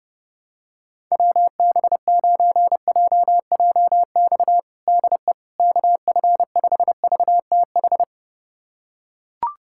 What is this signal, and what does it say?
Morse code, heard as a keyed tone, about 30 wpm, 700 Hz. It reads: WB9JJX DE KF5VTH
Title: USA Callsign DE Callsign - 6 Characters - 30wpm